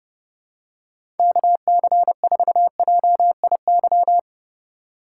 Morse code, heard as a keyed tone, about 30 wpm, 700 Hz. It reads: KC4JIY
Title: USA Call Signs - 6 Characters - 4x Character Spacing - 30wpm